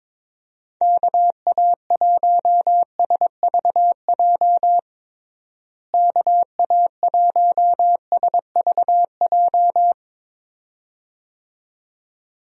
Morse code, heard as a keyed tone, about 22 wpm, 700 Hz. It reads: KA1SVJ KA1SVJ
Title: USA Call Signs - 6 Characters - Repeated 2x - 22wpm